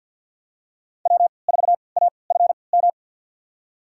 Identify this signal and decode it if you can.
Morse code, heard as a keyed tone, about 47 wpm, spaced as for 28 wpm, 700 Hz. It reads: W4AFM
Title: USA Call Signs - 5 Characters - 3x Character Spacing - 50wpm